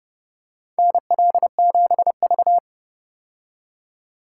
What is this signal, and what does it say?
Morse code, heard as a keyed tone, about 30 wpm, 700 Hz. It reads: NL7V
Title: K1USN SST QSO Practice - 30wpm